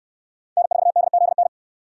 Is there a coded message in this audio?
Yes: Morse code from 0.6 to 1.5 s, heard as a keyed tone, about 68 wpm, 700 Hz. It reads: N4DZN